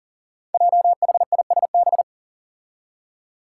Morse code, heard as a keyed tone, about 39 wpm, 700 Hz. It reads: JHISB
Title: Five Challenging Letter-Number Combinations 40wpm